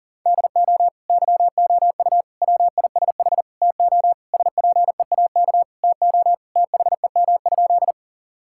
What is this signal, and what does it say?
Morse code, heard as a keyed tone, about 40 wpm, 700 Hz. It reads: DO YOU WISH TO SPEAK TO THEM?
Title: Sentences from Top 1600 Words 40wpm